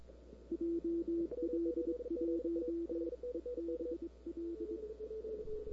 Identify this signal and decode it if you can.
Morse code, heard as a keyed tone, about 26 wpm, 330 Hz. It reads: JL1EBL
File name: CW / 2005-k5tr arrl cw dx -01:00